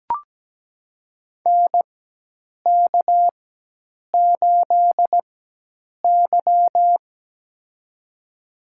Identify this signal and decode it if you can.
Morse code, heard as a keyed tone, about 17 wpm, spaced as for 8 wpm, 700 Hz. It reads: NK8Y
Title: USA Call Signs - 4 Characters - 4x Character Spacing - 17wpm